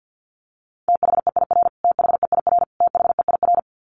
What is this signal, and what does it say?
Morse code, heard as a keyed tone, about 48 wpm, 700 Hz. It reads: THEIR THEIR THEIR
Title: Top 300 Words Repeated 50wpm